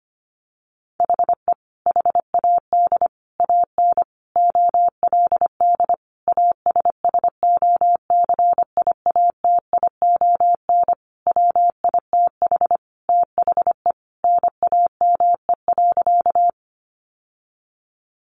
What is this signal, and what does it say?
Morse code, heard as a keyed tone, about 25 wpm, 700 Hz. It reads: HE HAD AN OLD ASSOCIATION WITH THE NAME.
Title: Sentences from Top 1100 Words 25wpm